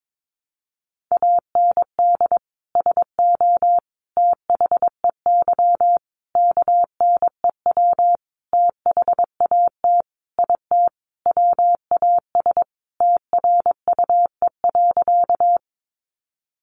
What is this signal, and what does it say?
Morse code, heard as a keyed tone, about 22 wpm, 700 Hz. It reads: AND SO THEY KNEW THAT IT WAS TRUE.
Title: Sentences from Top 400 Words 22wpm